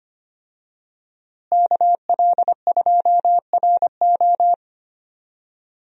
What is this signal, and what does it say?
Morse code, heard as a keyed tone, about 25 wpm, 700 Hz. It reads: KL2RO
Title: USA Call Signs - 5 Characters - Repeated 3x - 25wpm